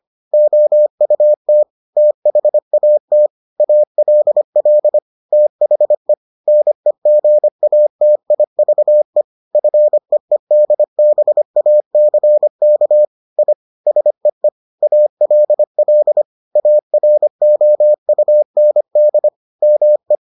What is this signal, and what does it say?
Morse code, heard as a keyed tone, about 25 wpm, 600 Hz. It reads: OUT THAT ALL THE NEGATIVE FEEDBACK I SEE ALL AROUND ME